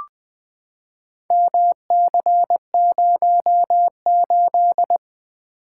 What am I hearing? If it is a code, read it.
Morse code, heard as a keyed tone, about 20 wpm, 700 Hz. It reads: MC08
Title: Four Letter-Number Combinations - 20wpm